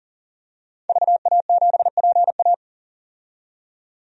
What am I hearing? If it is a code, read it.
Morse code, heard as a keyed tone, about 40 wpm, 700 Hz. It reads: VA7PA
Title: DXCC-100 Call Signs - 5 Characters - 40wpm